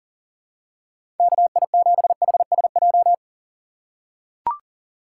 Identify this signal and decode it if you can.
Morse code, heard as a keyed tone, about 40 wpm, 700 Hz. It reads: KI7HSJ
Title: USA Call Signs - 6 Characters - 4x Character Spacing - 40wpm